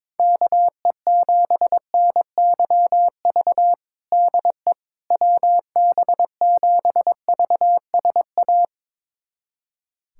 Morse code, heard as a keyed tone, about 21 wpm, 700 Hz. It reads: KE7NYV DE WB7VSA
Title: USA Callsign DE Callsign 22wpm